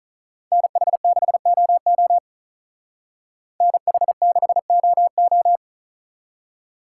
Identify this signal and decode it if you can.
Morse code, heard as a keyed tone, about 45 wpm, 700 Hz. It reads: NH6OO NH6OO